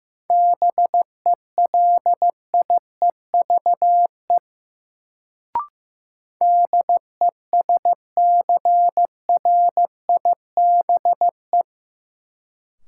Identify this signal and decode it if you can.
Morse code, heard as a keyed tone, about 15 wpm, 700 Hz. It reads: BELIEVE DESCRIBE
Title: Sentences from Top 1000 Words - New Word Drill - 15wpm